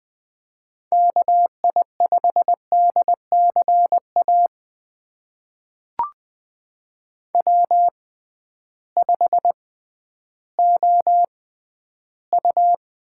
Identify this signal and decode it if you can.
Morse code, heard as a keyed tone, about 20 wpm, spaced as for 12 wpm, 700 Hz. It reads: KI5DCA W5OU
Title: USA Call Signs - 6x Character Spacing - 20wpm